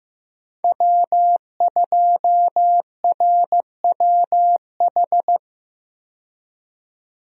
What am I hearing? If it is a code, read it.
Morse code, heard as a keyed tone, about 15 wpm, 700 Hz. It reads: W2RWH
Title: USA Call Signs - 5 Characters - 15wpm